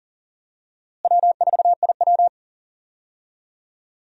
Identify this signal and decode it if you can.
Morse code, heard as a keyed tone, about 37 wpm, 700 Hz. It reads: W4IW
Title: USA Call Signs - 4 Characters - Repeated 2x - 40wpm